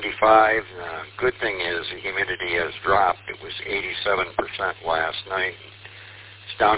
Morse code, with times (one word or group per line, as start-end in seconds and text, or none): none